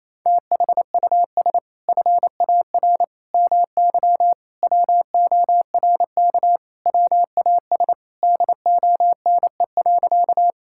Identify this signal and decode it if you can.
Morse code, heard as a keyed tone, about 28 wpm, 700 Hz. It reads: THUS FAR MY WORK WAS DONE.
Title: Sentences from Top 1000 Words - Review Part 5 - 28wpm